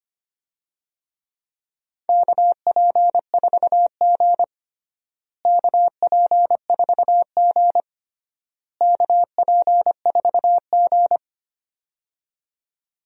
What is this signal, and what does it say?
Morse code, heard as a keyed tone, about 25 wpm, 700 Hz. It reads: KP4G KP4G KP4G
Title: DXCC-100 Call Signs - 4 Characters - Repeated 3x - 25wpm